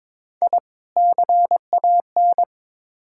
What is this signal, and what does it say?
Morse code, heard as a keyed tone, about 22 wpm, 700 Hz. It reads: I CAN
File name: practice / 22wpm / Build A Phrase - 5-Word Phrases - 22wpm